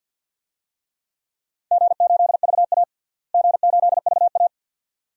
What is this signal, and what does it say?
Morse code, heard as a keyed tone, about 52 wpm, 700 Hz. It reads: G8VA G8VA
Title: DXCC-100 Call Signs - 4 Characters - Repeated 2x - 50wpm